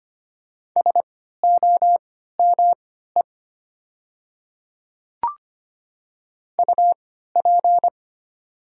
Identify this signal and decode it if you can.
Morse code, heard as a keyed tone, about 25 wpm, spaced as for 14 wpm, 700 Hz. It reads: SOME UP
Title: 3x Character Spacing - Top 100 Words - 25wpm